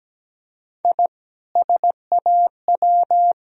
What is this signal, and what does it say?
Morse code, heard as a keyed tone, about 17 wpm, 700 Hz. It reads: I SAW
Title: Sentences from Top 1000 Words - Review Part 3 - 17wpm